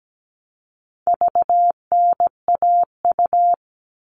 Morse code, heard as a keyed tone, about 17 wpm, 700 Hz. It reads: VNAU
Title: All Cut Numbers - 4-Digit 17wpm